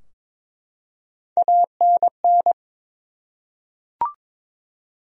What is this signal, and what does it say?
Morse code, heard as a keyed tone, about 22 wpm, 700 Hz. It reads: ANN
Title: USA Common Names - Max 3 Letters - 22wpm